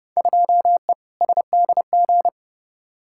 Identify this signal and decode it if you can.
Morse code, heard as a keyed tone, about 30 wpm, 700 Hz. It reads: 2E SDG
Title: Field Day Exchanges - 30wpm